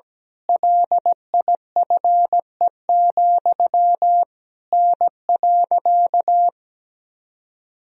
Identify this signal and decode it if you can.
Morse code, heard as a keyed tone, about 17 wpm, 700 Hz. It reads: LIFE, N.
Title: Sentences from Top 300 Words and Abbreviations 17wpm